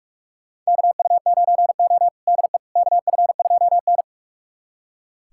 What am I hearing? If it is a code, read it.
Morse code, heard as a keyed tone, about 46 wpm, 700 Hz. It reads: KU9O DE KF2N
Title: USA Callsign DE Callsign - 4 Characters - 45wpm